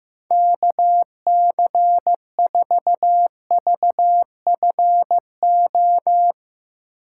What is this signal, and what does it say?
Morse code, heard as a keyed tone, about 15 wpm, 700 Hz. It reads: KC4VFO
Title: USA Call Signs - 6 Characters - 8x Character Spacing - 15wpm